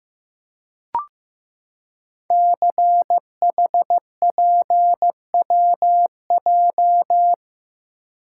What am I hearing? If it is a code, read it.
Morse code, heard as a keyed tone, about 15 wpm, 700 Hz. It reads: CHPWJ
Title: Code Groups - 5 Letters - 15wpm